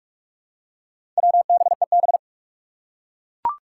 Morse code, heard as a keyed tone, about 41 wpm, 700 Hz. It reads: WBEB
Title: Four Challenging Letter-Number Combinations 45wpm